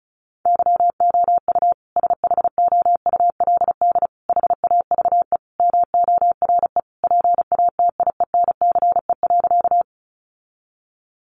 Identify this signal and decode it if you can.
Morse code, heard as a keyed tone, about 34 wpm, 700 Hz. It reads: YOU SHOULD HAVE MORE PATIENCE.
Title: Sentences from Top 1300 Words 35wpm